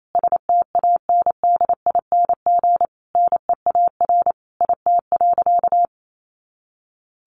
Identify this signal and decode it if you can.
Morse code, heard as a keyed tone, about 28 wpm, 700 Hz. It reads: STANDING NEAR IT.
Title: A Sound Of Thunder 28wpm